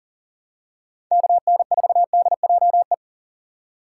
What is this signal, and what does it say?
Morse code, heard as a keyed tone, about 40 wpm, 700 Hz. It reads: KN4DJE
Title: USA Call Signs - 6 Characters - 2x Character Spacing - 40wpm